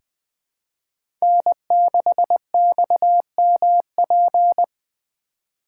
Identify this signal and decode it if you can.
Morse code, heard as a keyed tone, about 20 wpm, 700 Hz. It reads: N6XMP
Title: USA Call Signs - 5 Characters - 4x Character Spacing - 20wpm